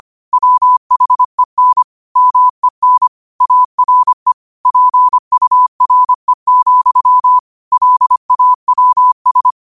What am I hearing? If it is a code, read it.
Morse code, heard as a keyed tone, about 25 wpm, 1000 Hz. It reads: WHEN MEN ARE PURE, LAWS
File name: qotdfiles / 25wpm